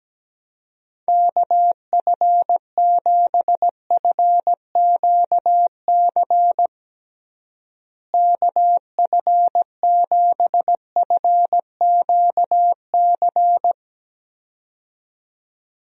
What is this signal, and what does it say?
Morse code, heard as a keyed tone, about 17 wpm, 700 Hz. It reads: KF7FQC KF7FQC
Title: USA Call Signs - 6 Characters - Repeated 2x - 17wpm